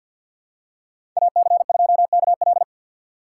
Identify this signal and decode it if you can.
Morse code, heard as a keyed tone, about 49 wpm, 700 Hz. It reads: AC2KL
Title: USA Call Signs - 4x Character Spacing - 50wpm